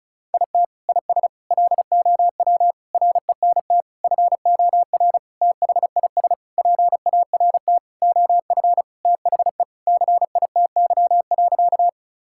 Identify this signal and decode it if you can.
Morse code, heard as a keyed tone, about 36 wpm, 700 Hz. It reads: IT IS LOW RENT FOR THIS PART OF THE CITY.
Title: Sentences from Top 1000 Words - Review Part 5 - 35wpm